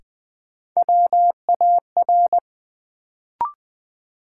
Morse code, heard as a keyed tone, about 20 wpm, 700 Hz. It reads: WAR
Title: Sets of 5 Words - New Word Drill - 20wpm